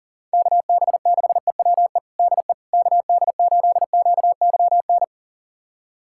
Morse code, heard as a keyed tone, about 40 wpm, 700 Hz. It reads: KB6EWE DE KD8QYN
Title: USA Callsign DE Callsign - 6 Characters - 40wpm